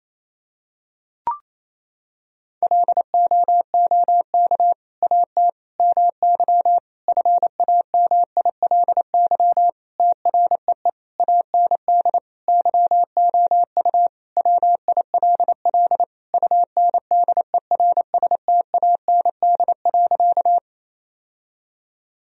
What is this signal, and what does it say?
Morse code, heard as a keyed tone, about 28 wpm, 700 Hz. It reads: LOOK AT MY FAMILY TREE AND YOU WILL UNDERSTAND.
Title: Sentences from Top 1000 Words - Review Part 1 - 28wpm